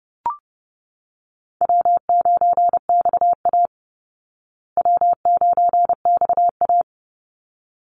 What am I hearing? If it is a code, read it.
Morse code, heard as a keyed tone, about 30 wpm, 700 Hz. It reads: W9XA W9XA
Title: Call Signs Repeated 3x - 30wpm